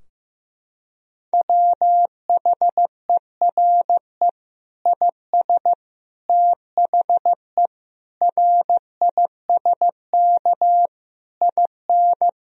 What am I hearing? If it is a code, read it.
Morse code, heard as a keyed tone, about 15 wpm, 700 Hz. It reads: WHERE IS THE RISK IN
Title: Sentences from Top 1000 Words - Review Part 4 - 15wpm